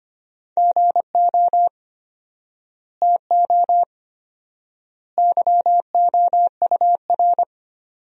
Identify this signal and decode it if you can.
Morse code, heard as a keyed tone, about 25 wpm, 700 Hz. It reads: GO TO YOUR
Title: 4x Word-Spacing - Sentences from Top 300 Words - 25wpm